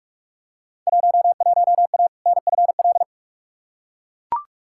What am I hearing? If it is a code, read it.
Morse code, heard as a keyed tone, about 42 wpm, 700 Hz. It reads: 11A NFL
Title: Field Day Exchanges Encore - 45wpm